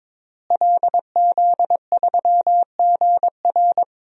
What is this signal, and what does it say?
Morse code, heard as a keyed tone, about 22 wpm, 700 Hz. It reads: LZ3GR
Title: DXCC-100 Call Signs - 5 Characters - Repeated 3x - 22wpm